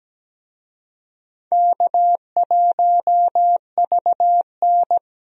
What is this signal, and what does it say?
Morse code, heard as a keyed tone, about 17 wpm, 700 Hz. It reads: K1VN
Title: USA Call Signs - 4 Characters - 17wpm